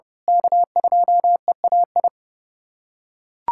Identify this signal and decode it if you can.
Morse code, heard as a keyed tone, about 30 wpm, 700 Hz. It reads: K2EAI
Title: USA Callsign DE Callsign - 5 Characters - 30wpm